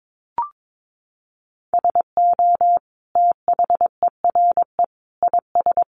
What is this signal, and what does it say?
Morse code, heard as a keyed tone, about 22 wpm, 700 Hz. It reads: SO THERE IS
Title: Sentences from Top 1500 Words 22wpm